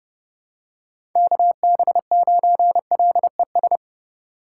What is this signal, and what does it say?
Morse code, heard as a keyed tone, about 30 wpm, 700 Hz. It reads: KB9LES
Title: USA Call Signs - 6 Characters - 6x Character Spacing - 30wpm